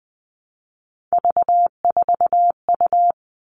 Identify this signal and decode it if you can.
Morse code, heard as a keyed tone, about 20 wpm, 700 Hz. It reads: V4U
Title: All Cut Numbers - 3-Digit 20wpm